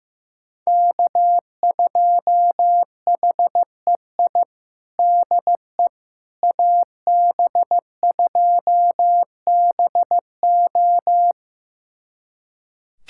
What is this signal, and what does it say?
Morse code, heard as a keyed tone, about 15 wpm, 700 Hz. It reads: K2HEI DE AB2BO
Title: USA Callsign DE Callsign - 5 Characters - 15wpm